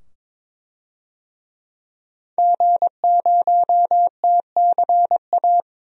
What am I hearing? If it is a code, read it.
Morse code, heard as a keyed tone, about 22 wpm, 700 Hz. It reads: G0TCA
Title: DXCC-100 Call Signs - 5 Characters - 22wpm